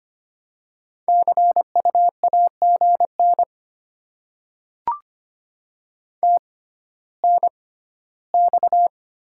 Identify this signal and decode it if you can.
Morse code, heard as a keyed tone, about 25 wpm, spaced as for 14 wpm, 700 Hz. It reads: CU AGN TNX
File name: Top 20 QSO Elements - 6x Character Spacing - 25wpm